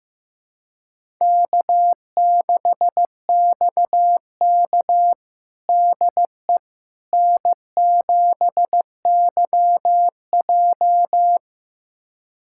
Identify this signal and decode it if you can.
Morse code, heard as a keyed tone, about 15 wpm, 700 Hz. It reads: K6XK DE N7YJ
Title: USA Callsign DE Callsign - 4 Characters - 15wpm